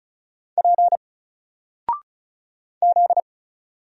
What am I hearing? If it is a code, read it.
Morse code, heard as a keyed tone, about 36 wpm, 700 Hz. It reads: P Z
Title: Single Character (Letter:Number:Punctuation:Prosign) Alt 35wpm